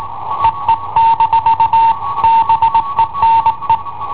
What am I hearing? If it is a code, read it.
Morse code, heard as a keyed tone, about 18 wpm, 910 Hz. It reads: EE- BENE
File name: i7ohp / cassetta i7ohp B